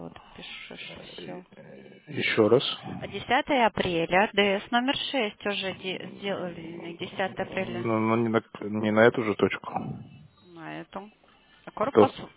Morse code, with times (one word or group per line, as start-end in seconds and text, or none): none